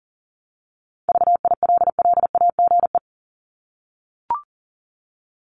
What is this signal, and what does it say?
Morse code, heard as a keyed tone, about 38 wpm, 700 Hz. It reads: VILLAGE
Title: Top 1751-2000 Words 40wpm